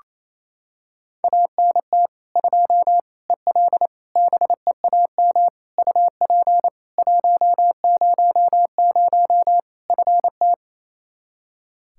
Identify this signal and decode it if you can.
Morse code, heard as a keyed tone, about 28 wpm, 700 Hz. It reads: ANT 2 EL BEAM UP 100 FT
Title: Practice QSO - Exchange 2 - Info - 28wpm